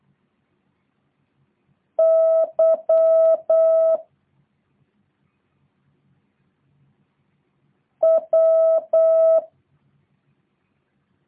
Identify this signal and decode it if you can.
Morse code, heard as a keyed tone, about 8 wpm, 650 Hz. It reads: Y W